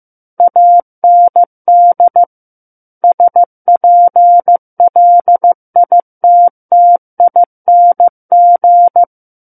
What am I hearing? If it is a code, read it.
Morse code, heard as a keyed tone, about 15 wpm, 700 Hz. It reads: AND SPLITTING
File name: Chapter 6 - 15 WPM